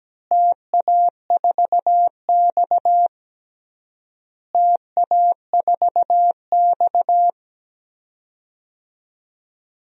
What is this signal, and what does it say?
Morse code, heard as a keyed tone, about 17 wpm, 700 Hz. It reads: TA4X TA4X
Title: DXCC-100 Call Signs - 4 Characters - Repeated 3x - 17wpm